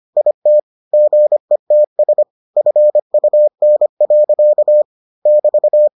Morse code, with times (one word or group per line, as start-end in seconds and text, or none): 0.2-0.6 IT
0.9-2.2 GETS
2.6-4.8 FUN.
5.3-5.9 <BT>